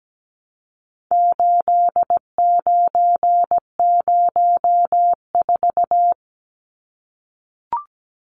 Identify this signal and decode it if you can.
Morse code, heard as a keyed tone, about 17 wpm, 700 Hz. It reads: 8904
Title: Numbers - 4-Digit 17wpm